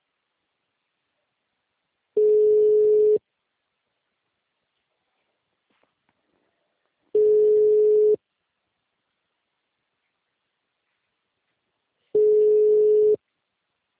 Ringback tone (Europe, 425 Hz).